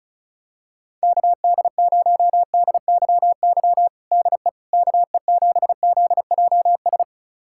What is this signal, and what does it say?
Morse code, heard as a keyed tone, about 36 wpm, 700 Hz. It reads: KD0DYY DE KE7ZJS
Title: USA Callsign DE Callsign - 6 Characters - 35wpm